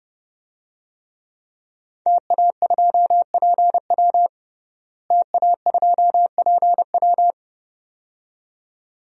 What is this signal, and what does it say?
Morse code, heard as a keyed tone, about 30 wpm, 700 Hz. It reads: TA2PW TA2PW
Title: DXCC-100 Call Signs - 5 Characters - Repeated 2x - 30wpm